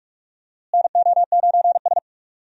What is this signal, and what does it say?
Morse code, heard as a keyed tone, about 46 wpm, 700 Hz. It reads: NO9S
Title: USA Call Signs - 4 Characters - Repeated 3x - 45wpm